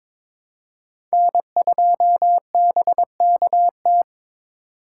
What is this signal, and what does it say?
Morse code, heard as a keyed tone, about 21 wpm, 700 Hz. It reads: N2BKT
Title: USA Call Signs - 5 Characters - 3x Character Spacing - 22wpm